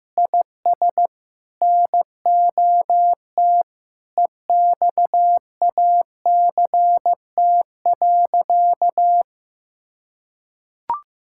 Morse code, heard as a keyed tone, about 16 wpm, 700 Hz. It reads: IS NOT EXACT.
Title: Sentences from Top 1600 Words 15wpm